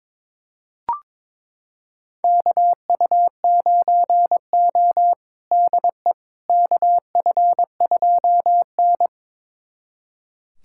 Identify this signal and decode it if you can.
Morse code, heard as a keyed tone, about 22 wpm, 700 Hz. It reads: KU9O DE KF2N
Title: USA Callsign DE Callsign - 4 Characters - 22wpm